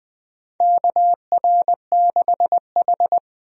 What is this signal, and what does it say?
Morse code, heard as a keyed tone, about 21 wpm, 700 Hz. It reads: KR6H
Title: USA Call Signs - 4 Characters - Repeated 3x - 20wpm